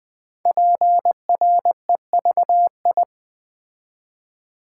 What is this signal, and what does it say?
Morse code, heard as a keyed tone, about 20 wpm, 700 Hz. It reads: PREVI
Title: Build A Word - 5 Letters At A Time - 10-Letter Words - 20wpm